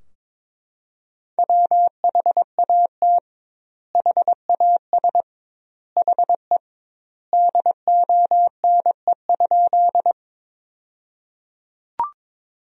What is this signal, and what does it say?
Morse code, heard as a keyed tone, about 22 wpm, 700 Hz. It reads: WHAT HAS HE DONE?
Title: 2x Word-Spacing - Sentences from Top 500 Words - 22wpm